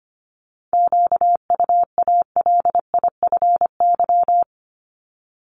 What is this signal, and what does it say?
Morse code, heard as a keyed tone, about 25 wpm, 700 Hz. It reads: QUALIFY
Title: Top 1751-2000 Words 25wpm